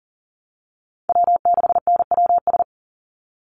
Morse code, heard as a keyed tone, about 40 wpm, 700 Hz. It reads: W6NWS
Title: Call Signs - 40wpm